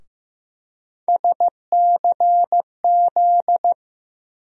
Morse code, heard as a keyed tone, about 15 wpm, 700 Hz. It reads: SCZ
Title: USA Call Sign Postfixes - 3 Letters 15wpm